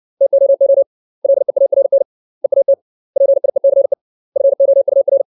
Morse code, heard as a keyed tone, about 60 wpm, 550 Hz. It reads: TOO LEARN EAT PIECE FOUR
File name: 632 words / 632 Words 60WPM